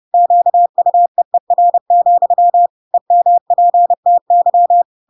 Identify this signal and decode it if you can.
Morse code, heard as a keyed tone, about 30 wpm, 700 Hz. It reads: QUEER, EMPTY